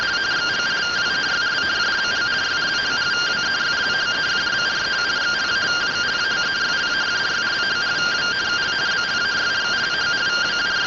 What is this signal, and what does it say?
Two-tone teleprinter signal, about 170 Hz shift.